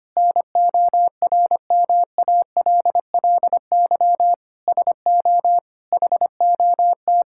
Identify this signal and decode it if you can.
Morse code, heard as a keyed tone, about 25 wpm, 700 Hz. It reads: NORMALLY SO HOT
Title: Sentences from Top 1000 Words - Review Part 6 - 25wpm